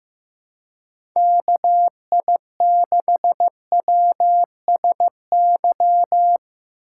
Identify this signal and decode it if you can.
Morse code, heard as a keyed tone, about 15 wpm, 700 Hz. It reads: KI6WSY
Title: USA Call Signs - 6 Characters - 3x Character Spacing - 15wpm